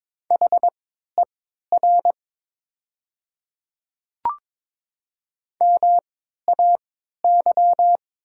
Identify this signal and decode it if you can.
Morse code, heard as a keyed tone, about 22 wpm, spaced as for 13 wpm, 700 Hz. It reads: HER MAY
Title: 3x Character Spacing - Top 500 Words - 22wpm